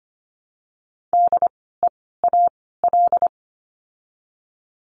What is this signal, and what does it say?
Morse code, heard as a keyed tone, about 25 wpm, 700 Hz. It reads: D E A L
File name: course / 25:10wpm / 094 Lesson 16f - Letter c - Words 25wpm